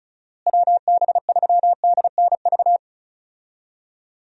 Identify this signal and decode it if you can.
Morse code, heard as a keyed tone, about 36 wpm, 700 Hz. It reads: WB3DNV
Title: USA Call Signs - 6 Characters - Repeated 2x - 35wpm